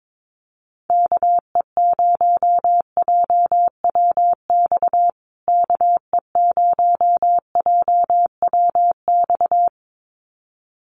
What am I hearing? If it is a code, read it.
Morse code, heard as a keyed tone, about 22 wpm, 700 Hz. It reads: KE0JWX KE0JWX
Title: USA Call Signs Repeated 2x 22wpm